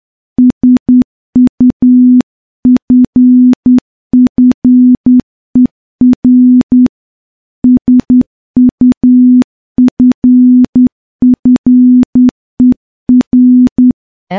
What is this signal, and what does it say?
Morse code, heard as a keyed tone, about 10 wpm, 250 Hz. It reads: SUFFER SUFFER
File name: morse / morse-training-10WPM